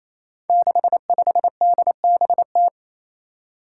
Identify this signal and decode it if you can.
Morse code, heard as a keyed tone, about 28 wpm, 700 Hz. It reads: 65DBT
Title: Five Challenging Letter-Number Combinations 28wpm